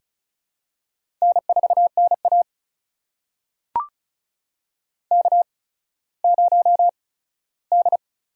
Morse code, heard as a keyed tone, about 36 wpm, spaced as for 17 wpm, 700 Hz. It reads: N4NA K0D
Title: USA Call Signs - 4 Characters - 8x Character Spacing - 35wpm